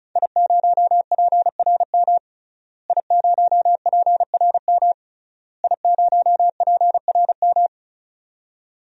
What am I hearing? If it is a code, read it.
Morse code, heard as a keyed tone, about 34 wpm, 700 Hz. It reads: I0PRM I0PRM I0PRM
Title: DXCC-100 Call Signs - 5 Characters - Repeated 3x - 35wpm